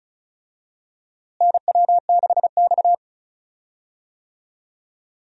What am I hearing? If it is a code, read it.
Morse code, heard as a keyed tone, about 36 wpm, 700 Hz. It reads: NW6X
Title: USA Call Signs - 4 Characters - Repeated 2x - 35wpm